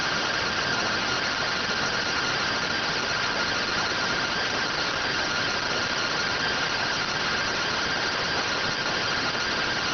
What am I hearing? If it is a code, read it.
Two-tone teleprinter signal, about 170 Hz shift.